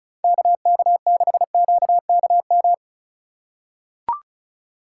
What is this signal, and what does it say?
Morse code, heard as a keyed tone, about 34 wpm, 700 Hz. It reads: KK6QKM
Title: USA Call Signs - 6 Characters - 2x Character Spacing - 35wpm